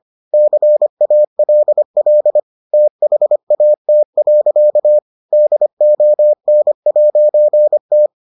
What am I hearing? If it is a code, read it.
Morse code, heard as a keyed tone, about 25 wpm, 600 Hz. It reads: CALL THAT. DON'T